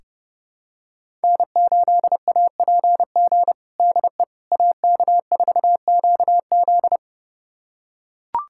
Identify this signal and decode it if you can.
Morse code, heard as a keyed tone, about 29 wpm, 700 Hz. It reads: N8APG DE AK4QZ
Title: USA Callsign DE Callsign - 5 Characters - 30wpm